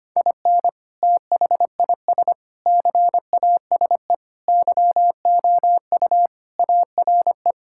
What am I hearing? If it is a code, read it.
Morse code, heard as a keyed tone, about 25 wpm, 700 Hz. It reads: IN THIS CASE YOU ARE
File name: practice / 25wpm / Sentences with 3-Word Phrases - 25wpm